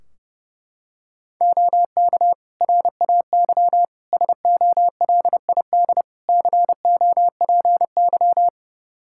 Morse code, heard as a keyed tone, about 30 wpm, 700 Hz. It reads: OK RAY SOLID COPY
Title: Practice QSO - Exchange 3 - More Info - 30wpm